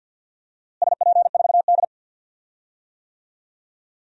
Morse code, heard as a keyed tone, about 47 wpm, 700 Hz. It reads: SP4D
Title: DXCC-100 Call Signs - 4 Characters - Repeated 3x - 50wpm